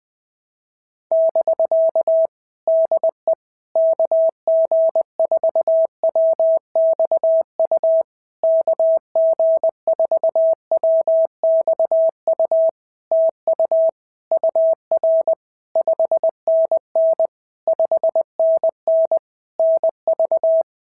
Morse code, heard as a keyed tone, about 20 wpm, 650 Hz. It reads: <BK> DE KG4WXU KG4WXU TU UR 5NN 5NN NV
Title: POTA QSO Practice - 20wpm